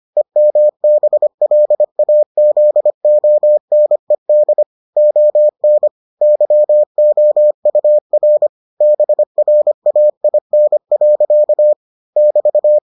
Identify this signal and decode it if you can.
Morse code, heard as a keyed tone, about 25 wpm, 600 Hz. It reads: EMBLAZONED ON YOUR BRAIN. <BT>